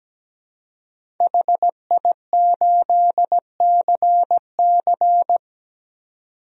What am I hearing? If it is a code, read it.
Morse code, heard as a keyed tone, about 17 wpm, 700 Hz. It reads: HI8CC
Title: DXCC-100 Call Signs - 5 Characters - 17wpm